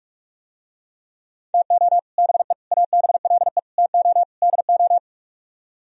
Morse code, heard as a keyed tone, about 49 wpm, 700 Hz. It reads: TO BE ABLE TO DO